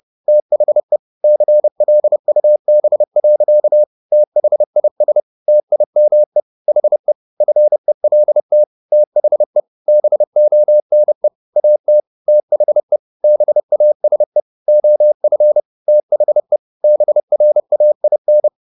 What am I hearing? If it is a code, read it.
Morse code, heard as a keyed tone, about 30 wpm, 600 Hz. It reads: THE CLUB. THIS TIME HE FELT THE BONE AT THE BASE OF THE BRAIN